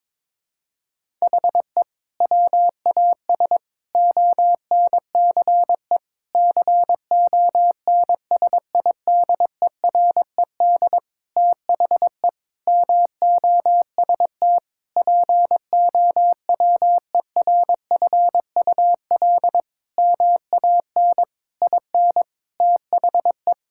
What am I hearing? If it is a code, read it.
Morse code, heard as a keyed tone, about 21 wpm, 700 Hz. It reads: HE WAS ONCE CONSIDERED THE MOST POWERFUL MAN IN THE
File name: Sentences with 7-Word Phrases - 22wpm